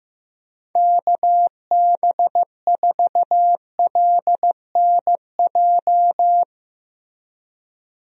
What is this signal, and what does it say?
Morse code, heard as a keyed tone, about 15 wpm, 700 Hz. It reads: KB4LNJ
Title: USA Call Signs - 6 Characters - Repeated 3x - 15wpm